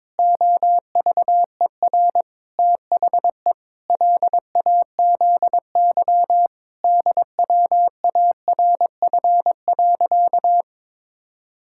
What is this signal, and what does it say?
Morse code, heard as a keyed tone, about 22 wpm, 700 Hz. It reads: OVER THE LAZY DWARF.
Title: Pangram Challenge v2 - 22wpm